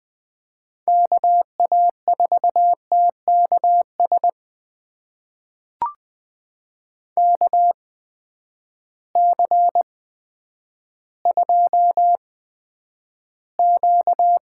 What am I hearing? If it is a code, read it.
Morse code, heard as a keyed tone, about 20 wpm, spaced as for 10 wpm, 700 Hz. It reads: KA4TKS KC2Q
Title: USA Call Signs - 6 Characters - 8x Character Spacing - 20wpm